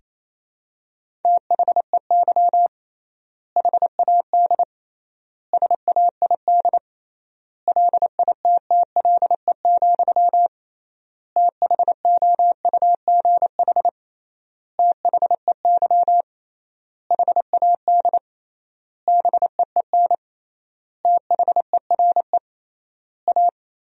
Morse code, heard as a keyed tone, about 28 wpm, 700 Hz. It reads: THEY HAD SAID LITTLE, THOUGH THEY HAD BEEN THERE A
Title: 3x Word-Spacing - Sentences from Top 400 Words - 28wpm